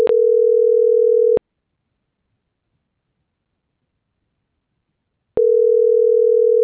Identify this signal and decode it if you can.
Ringback tone (North America), 440 + 480 Hz.